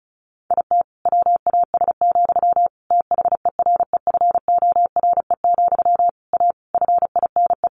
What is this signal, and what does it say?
Morse code, heard as a keyed tone, about 34 wpm, 700 Hz. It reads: IT WAS, THEREFORE, A FINE
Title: Sentences from Top 1400 Words 35wpm